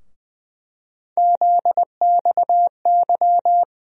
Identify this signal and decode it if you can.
Morse code, heard as a keyed tone, about 20 wpm, 700 Hz. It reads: ZXY
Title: USA Call Sign Postfixes - 3 Letters 20wpm